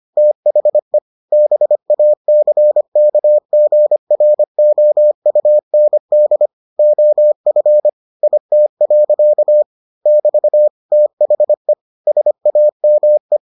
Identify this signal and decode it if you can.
Morse code, heard as a keyed tone, about 25 wpm, 600 Hz. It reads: THE BACKGROUND OF IT. <BT> THE SAME